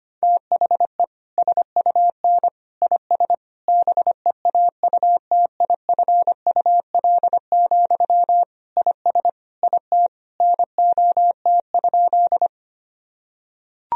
Morse code, heard as a keyed tone, about 25 wpm, 700 Hz. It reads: THE SUN IS BEAUTIFUL, IS IT NOT?
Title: Sentences from Top 1000 Words - Review Part 1 - 25wpm